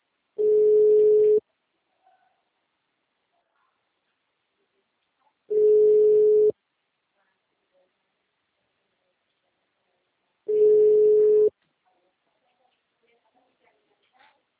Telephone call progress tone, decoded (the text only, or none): ringback tone